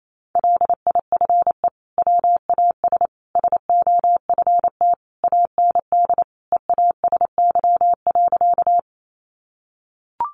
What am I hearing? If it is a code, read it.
Morse code, heard as a keyed tone, about 28 wpm, 700 Hz. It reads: LIFE WAS SOFT AND EASY.
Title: Sentences from Top 1500 Words 28wpm